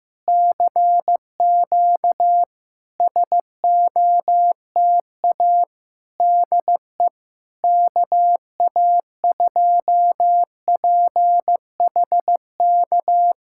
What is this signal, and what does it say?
Morse code, heard as a keyed tone, about 15 wpm, 700 Hz. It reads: CQ SOTA DE KA2PHK
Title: SOTA QSO Practice 15wpm